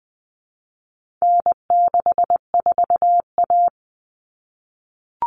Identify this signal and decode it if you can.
Morse code, heard as a keyed tone, about 20 wpm, 700 Hz. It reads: N64A
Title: Common Cut Numbers - 4-Digit 20wpm